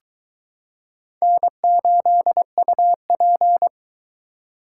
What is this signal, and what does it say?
Morse code, heard as a keyed tone, about 22 wpm, 700 Hz. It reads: N8UP